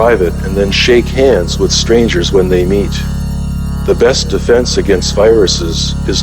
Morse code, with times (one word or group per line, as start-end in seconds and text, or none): none